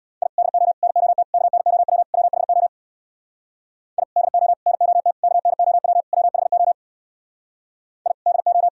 Morse code, heard as a keyed tone, about 62 wpm, 700 Hz. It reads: I DO NOT KNOW WHO I DO NOT KNOW WHO I DO
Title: Speed-Racing - Sets of 5 Words 50wpm